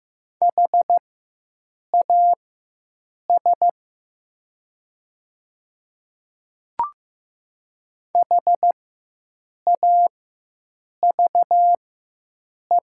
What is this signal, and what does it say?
Morse code, heard as a keyed tone, about 15 wpm, spaced as for 7 wpm, 700 Hz. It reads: HAS HAVE
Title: 4x Character Spacing - Top 200 Words - 15wpm